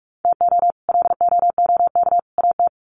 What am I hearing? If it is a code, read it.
Morse code, heard as a keyed tone, about 44 wpm, 700 Hz. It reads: TO LOOK AT
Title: Sets of 3 Words 45wpm